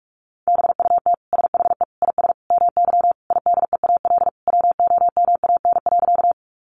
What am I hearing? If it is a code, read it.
Morse code, heard as a keyed tone, about 44 wpm, 700 Hz. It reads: BUT SHE IS MY IDEAL WOMAN.
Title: Sentences from Top 1400 Words 45wpm